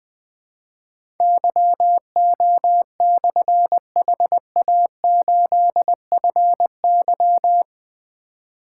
Morse code, heard as a keyed tone, about 20 wpm, 700 Hz. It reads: YO/HA8FY
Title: Difficult Call Signs 20wpm